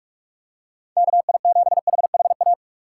Morse code, heard as a keyed tone, about 44 wpm, 700 Hz. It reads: KI7HHA
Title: USA Call Signs - 6 Characters - 6x Character Spacing - 45wpm